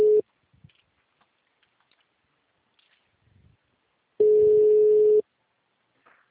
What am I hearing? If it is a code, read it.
Ringback tone (Europe, 425 Hz).